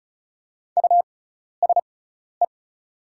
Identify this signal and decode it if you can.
Morse code, heard as a keyed tone, about 34 wpm, spaced as for 12 wpm, 690 Hz. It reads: USE
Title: 6x Character Spacing - Top 100 Words - 35wpm